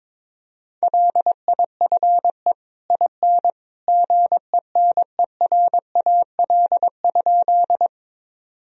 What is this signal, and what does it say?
Morse code, heard as a keyed tone, about 22 wpm, 700 Hz. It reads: LIFE IN GENERAL?
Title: Sentences from Top 1000 Words - Review Part 3 - 22wpm